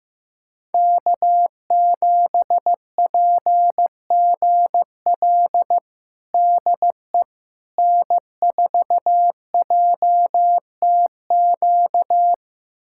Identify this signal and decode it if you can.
Morse code, heard as a keyed tone, about 15 wpm, 700 Hz. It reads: K7PGL DE N4JTQ
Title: USA Callsign DE Callsign - 5 Characters - 15wpm